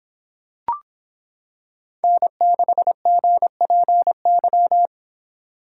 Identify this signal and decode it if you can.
Morse code, heard as a keyed tone, about 25 wpm, 700 Hz. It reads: N6GPY